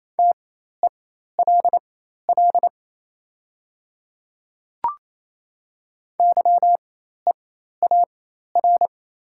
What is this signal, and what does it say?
Morse code, heard as a keyed tone, about 28 wpm, spaced as for 13 wpm, 700 Hz. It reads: TELL YEAR
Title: 4x Character Spacing - Top 300 Words - 28wpm